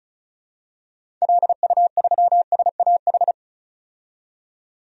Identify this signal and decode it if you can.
Morse code, heard as a keyed tone, about 34 wpm, 700 Hz. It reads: LU3SAH
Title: DXCC-100 Call Signs - 6 Characters - 35wpm